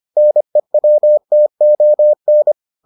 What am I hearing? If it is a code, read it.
Morse code, heard as a keyed tone, about 25 wpm, 600 Hz. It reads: NEWTON